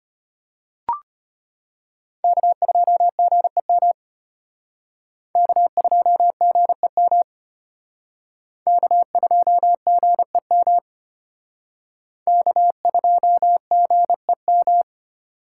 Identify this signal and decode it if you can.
Morse code, heard as a keyed tone, about 31 wpm, 700 Hz. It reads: K2GEM K2GEM K2GEM K2GEM
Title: Speed Racing - USA 1x3 Call Signs 25wpm